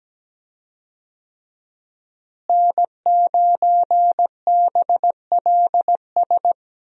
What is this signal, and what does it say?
Morse code, heard as a keyed tone, about 17 wpm, 700 Hz. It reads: N9BLS
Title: USA Call Signs - 5 Characters - Repeated 2x - 17wpm